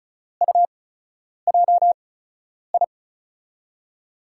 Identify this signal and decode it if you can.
Morse code, heard as a keyed tone, about 36 wpm, spaced as for 10 wpm, 700 Hz. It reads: UJI
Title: USA Call Signs - 5 Characters - 8x Character Spacing - 35wpm